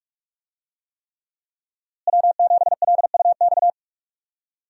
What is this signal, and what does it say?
Morse code, heard as a keyed tone, about 45 wpm, 700 Hz. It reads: W7LUX
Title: USA Call Signs - 5 Characters - Repeated 3x - 45wpm